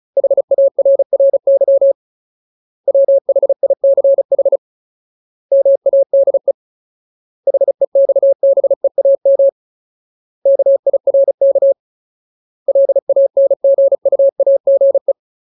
Morse code, heard as a keyed tone, about 34 wpm, 550 Hz. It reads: HARRY WHICH MADE HEXBEAM KIRK LANGUAGE
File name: Ultimate Word List / 35WPM Ultimate